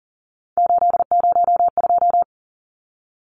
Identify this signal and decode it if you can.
Morse code, heard as a keyed tone, about 40 wpm, 700 Hz. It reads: 802
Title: Numbers - 3-Digit 40wpm